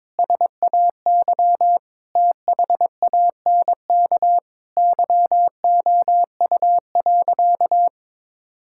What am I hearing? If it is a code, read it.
Morse code, heard as a keyed tone, about 22 wpm, 700 Hz. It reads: SAY THANK YOU.
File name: practice / 22wpm / Sentences with 2-Word Phrases - 22wpm